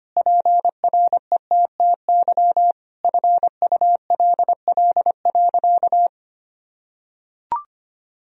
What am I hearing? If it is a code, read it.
Morse code, heard as a keyed tone, about 25 wpm, 700 Hz. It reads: PRETTY FULL.
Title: Sentences from Top 1000 Words - Review Part 3 - 25wpm